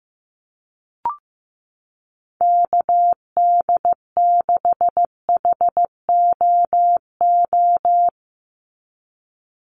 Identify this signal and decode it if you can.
Morse code, heard as a keyed tone, about 15 wpm, 700 Hz. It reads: KD6HOO
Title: USA Call Signs 15wpm